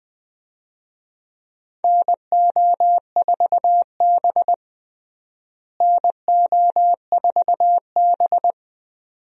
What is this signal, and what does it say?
Morse code, heard as a keyed tone, about 20 wpm, 700 Hz. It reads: NO4B NO4B
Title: USA Call Signs - 4 Characters - Repeated 2x - 20wpm